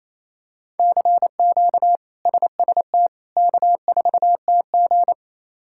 Morse code, heard as a keyed tone, about 27 wpm, 700 Hz. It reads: CQ SST K4TG
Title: K1USN SST QSO Practice - 28wpm